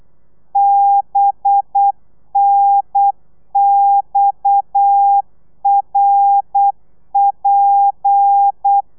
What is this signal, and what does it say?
Morse code, heard as a keyed tone, about 8 wpm, 800 Hz. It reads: BNXRP